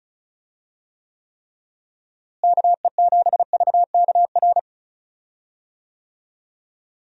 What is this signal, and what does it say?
Morse code, heard as a keyed tone, about 34 wpm, 700 Hz. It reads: KE7VKR
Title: USA Call Signs - 6 Characters - Repeated 3x - 35wpm